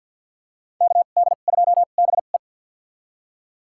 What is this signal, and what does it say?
Morse code, heard as a keyed tone, about 51 wpm, spaced as for 36 wpm, 700 Hz. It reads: KD2BE
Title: USA Call Signs - 5 Characters - 2x Character Spacing - 50wpm